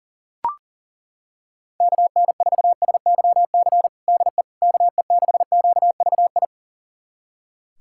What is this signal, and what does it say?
Morse code, heard as a keyed tone, about 40 wpm, 700 Hz. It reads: KN4SYC DE KE6QVI
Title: USA Callsign DE Callsign - 6 Characters - 40wpm